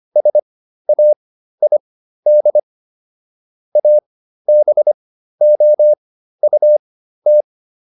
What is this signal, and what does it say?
Morse code, heard as a keyed tone, about 25 wpm, spaced as for 13 wpm, 600 Hz. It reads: SAID ABOUT